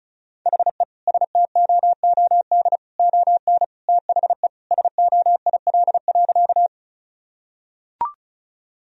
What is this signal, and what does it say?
Morse code, heard as a keyed tone, about 36 wpm, 700 Hz. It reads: HE STOOD ON THE SOIL.
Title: Sentences from Top 1000 Words - Review Part 4 - 35wpm